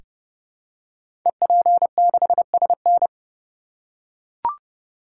Morse code, heard as a keyed tone, about 30 wpm, 700 Hz. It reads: EP6SN
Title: Code Groups - 5 Letter:Numbers - 30wpm